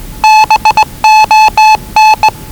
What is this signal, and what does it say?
Morse code, heard as a keyed tone, about 20 wpm, 840 Hz. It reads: BON
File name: WellingtonRadioZlwCloseDownOnHfAndMf / 01-Zlw 4277khz 417.5 qrt until 0925z